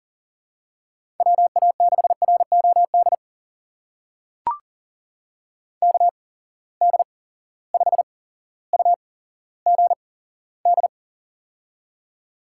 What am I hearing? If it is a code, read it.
Morse code, heard as a keyed tone, about 40 wpm, spaced as for 17 wpm, 700 Hz. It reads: WA6ROD KD5UGD
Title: USA Call Signs - 6 Characters - 8x Character Spacing - 40wpm